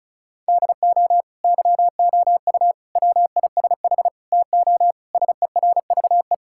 Morse code, heard as a keyed tone, about 36 wpm, 700 Hz. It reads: DO YOU WISH TO SERVE
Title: Sentences from Top 1600 Words 35wpm